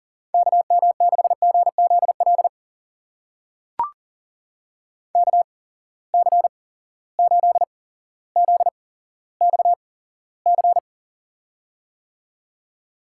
Morse code, heard as a keyed tone, about 42 wpm, spaced as for 17 wpm, 700 Hz. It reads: KM6GZL KC8ZXC
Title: USA Call Signs - 6 Characters - 8x Character Spacing - 40wpm